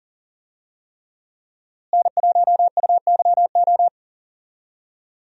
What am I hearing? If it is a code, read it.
Morse code, heard as a keyed tone, about 40 wpm, 700 Hz. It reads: N1UYO
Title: USA Call Signs - 5 Characters - Repeated 3x - 40wpm